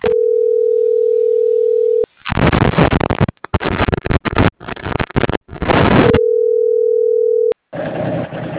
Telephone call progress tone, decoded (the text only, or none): ringback tone